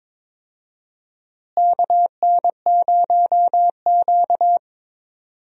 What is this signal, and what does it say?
Morse code, heard as a keyed tone, about 22 wpm, 700 Hz. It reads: KN0Q
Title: USA Call Signs - 4 Characters - Repeated 3x - 22wpm